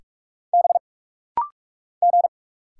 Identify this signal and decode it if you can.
Morse code, heard as a keyed tone, about 45 wpm, 700 Hz. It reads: B G
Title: Single Letter-Number - Rapid-Fire 45wpm